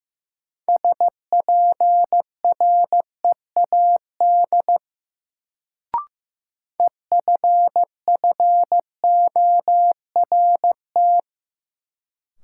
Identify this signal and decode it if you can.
Morse code, heard as a keyed tone, about 16 wpm, 700 Hz. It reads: SPREAD EFFORT
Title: Sentences from Top 1200 Words - New Word Drill Alt - 15wpm